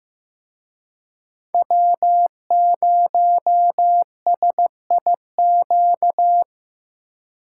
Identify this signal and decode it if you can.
Morse code, heard as a keyed tone, about 17 wpm, 700 Hz. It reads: W0SIQ
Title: Speed Racing - USA 1x3 Call Signs 15wpm